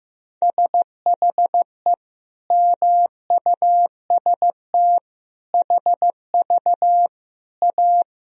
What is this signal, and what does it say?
Morse code, heard as a keyed tone, about 15 wpm, 700 Hz. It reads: SHE MUST HV A
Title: Sentences from Top 300 Words and Abbreviations 15wpm